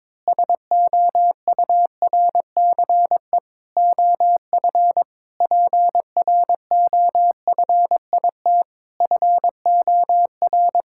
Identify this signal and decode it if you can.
Morse code, heard as a keyed tone, about 22 wpm, 700 Hz. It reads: SOURCE OF PROFIT FOR
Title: Sentences from Top 1000 Words - Review Part 5 - 22wpm